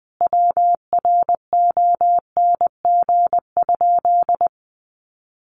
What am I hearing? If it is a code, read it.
Morse code, heard as a keyed tone, about 20 wpm, 700 Hz. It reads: WRONG?
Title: Sentences from Top 1100 Words 20wpm